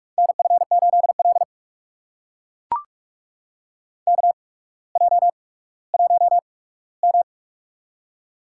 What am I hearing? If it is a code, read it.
Morse code, heard as a keyed tone, about 44 wpm, spaced as for 19 wpm, 700 Hz. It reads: NF8L KJ1M
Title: USA Call Signs - 4 Characters - 8x Character Spacing - 45wpm